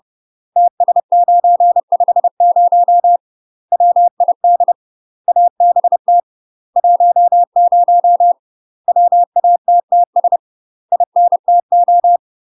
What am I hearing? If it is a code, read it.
Morse code, heard as a keyed tone, about 30 wpm, 700 Hz. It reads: TS950 WID ABT 10 WATTS INTO
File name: UK MASTER QSOs TRAINING / 30WPM / UK MASTER QSOs 0002